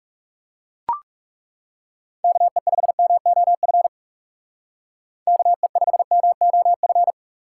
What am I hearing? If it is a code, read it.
Morse code, heard as a keyed tone, about 35 wpm, 700 Hz. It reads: KE5MOF KE5MOF
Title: Speed Racing - USA Call Signs 30wpm